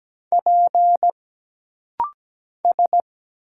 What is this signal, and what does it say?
Morse code, heard as a keyed tone, about 17 wpm, 700 Hz. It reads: P S
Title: Single Letters - Rapid-Fire 17wpm